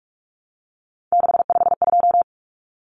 Morse code, heard as a keyed tone, about 44 wpm, 700 Hz. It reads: 652
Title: Numbers - 3-Digit 45wpm